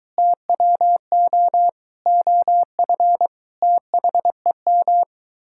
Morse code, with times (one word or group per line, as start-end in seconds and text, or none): 0.2-1.7 TWO
2.1-3.3 OF
3.6-5.0 THEM